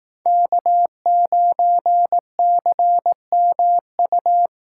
Morse code, heard as a keyed tone, about 18 wpm, 700 Hz. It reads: K9CMU
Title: Speed Racing - USA Call Signs 15wpm